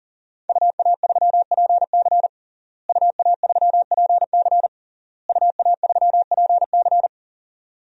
Morse code, heard as a keyed tone, about 40 wpm, 700 Hz. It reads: UA3PC UA3PC UA3PC
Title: DXCC-100 Call Signs - 5 Characters - Repeated 3x - 40wpm